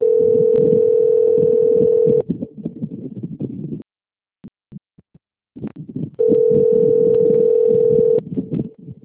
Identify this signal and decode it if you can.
Ringback tone (North America), 440 + 480 Hz.
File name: out- - - - - .67596